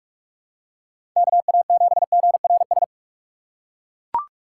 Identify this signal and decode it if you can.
Morse code, heard as a keyed tone, about 41 wpm, 700 Hz. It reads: KA7GRS
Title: USA Call Signs - 6x Character Spacing - 45wpm